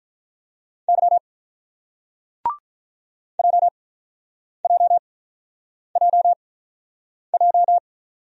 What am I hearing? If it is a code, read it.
Morse code, heard as a keyed tone, about 48 wpm, spaced as for 9 wpm, 700 Hz. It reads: Y JJJJ